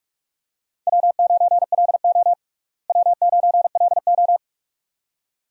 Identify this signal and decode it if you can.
Morse code, heard as a keyed tone, about 46 wpm, 700 Hz. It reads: W9LO W9LO
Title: USA Call Signs - 4 Characters - Repeated 2x - 45wpm